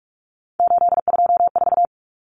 Morse code, heard as a keyed tone, about 44 wpm, 700 Hz. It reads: 824
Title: Common Cut Numbers - 3-Digit 45wpm